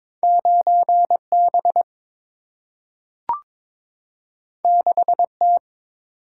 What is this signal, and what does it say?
Morse code, heard as a keyed tone, about 22 wpm, 700 Hz. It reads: 9 B 6 T
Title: Single-Letter Single-Number 22wpm